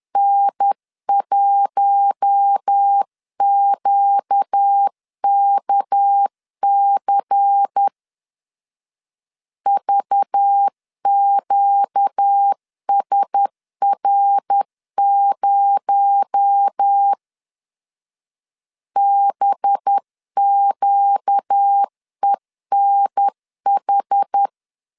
Morse code, heard as a keyed tone, about 10 wpm, 800 Hz. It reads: N1QKC VQSR0 BQENH
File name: CW / DET / 1 6 1500C S C 10wpm 4252